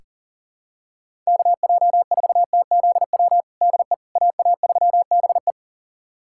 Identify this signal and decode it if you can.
Morse code, heard as a keyed tone, about 40 wpm, 700 Hz. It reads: KJ4TZW DE AA3BE
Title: USA Callsign DE Callsign 40wpm